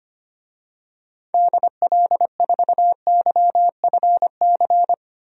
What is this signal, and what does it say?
Morse code, heard as a keyed tone, about 25 wpm, 700 Hz. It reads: DL4YFC
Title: DXCC-100 Call Signs - 6 Characters - 25wpm